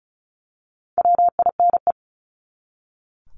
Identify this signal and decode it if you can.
Morse code, heard as a keyed tone, about 34 wpm, 700 Hz. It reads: WINE
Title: Top 1501-1750 Words 35wpm